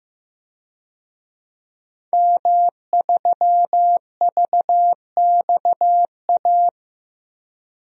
Morse code, heard as a keyed tone, about 16 wpm, 700 Hz. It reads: M3VXA
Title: DXCC-100 Call Signs - 5 Characters - Repeated 2x - 15wpm